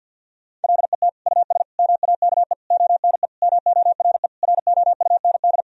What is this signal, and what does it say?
Morse code, heard as a keyed tone, about 50 wpm, 700 Hz. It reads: LET US MAKE ONE MORE ROUND
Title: Sentences from Top 1000 Words - Review Part 4 - 50wpm